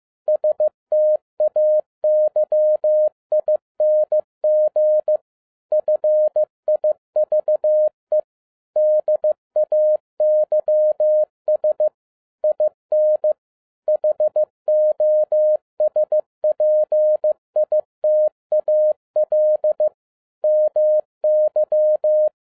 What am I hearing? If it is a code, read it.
Morse code, heard as a keyed tone, about 15 wpm, 610 Hz. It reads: STAYING FIVE DAYS IN HOSPITAL MY
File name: 2011-07-mp3 / n57 BACK FROM A BIG ACCIDENT 15wpm